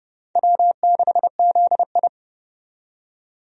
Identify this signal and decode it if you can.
Morse code, heard as a keyed tone, about 30 wpm, 700 Hz. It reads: W6ZI
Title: USA Call Signs - 4 Characters - 30wpm